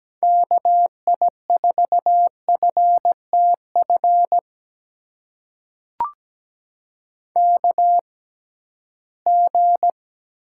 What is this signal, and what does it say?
Morse code, heard as a keyed tone, about 17 wpm, spaced as for 13 wpm, 700 Hz. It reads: KI4FTF KG
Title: USA Call Signs - 6 Characters - 6x Character Spacing - 17wpm